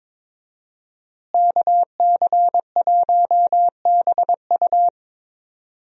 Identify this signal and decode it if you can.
Morse code, heard as a keyed tone, about 22 wpm, 700 Hz. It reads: KC1BU
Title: USA Call Signs - 5 Characters - Repeated 3x - 22wpm